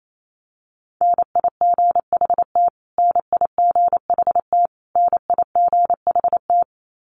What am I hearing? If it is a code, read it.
Morse code, heard as a keyed tone, about 28 wpm, 700 Hz. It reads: NIGHT NIGHT NIGHT
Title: Top 300 Words Repeated 28wpm